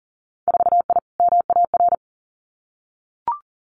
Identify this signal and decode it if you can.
Morse code, heard as a keyed tone, about 40 wpm, 700 Hz. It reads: 4I MAR
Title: Winter Field Day Exchanges Encore 40wpm